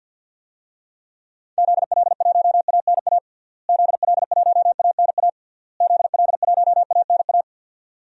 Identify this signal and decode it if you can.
Morse code, heard as a keyed tone, about 50 wpm, 700 Hz. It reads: ZL1ANA ZL1ANA ZL1ANA
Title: DXCC-100 Call Signs - 6 Characters - Repeated 3x - 50wpm